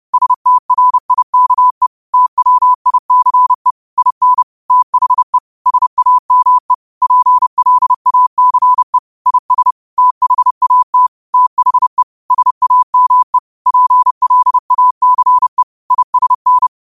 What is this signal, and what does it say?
Morse code, heard as a keyed tone, about 30 wpm, 1000 Hz. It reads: STRIKE TWICE IN THE SAME PLACE IS THAT THE SAME PLACE ISN
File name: qotdfiles / 30wpm